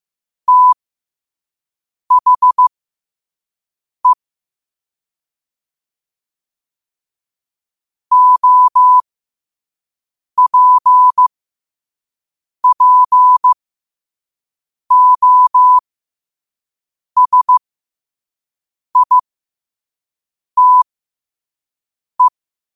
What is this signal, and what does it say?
Morse code, heard as a keyed tone, about 15 wpm, spaced as for 5 wpm, 1000 Hz. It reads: THE OPPOSITE
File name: qotdfiles / 5wpm